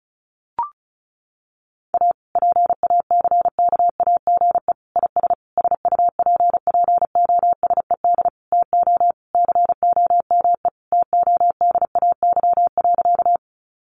Morse code, heard as a keyed tone, about 34 wpm, 700 Hz. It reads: A PACKAGE IS SUPPOSED TO COME TODAY.
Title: Additional Easy Phrases 35wpm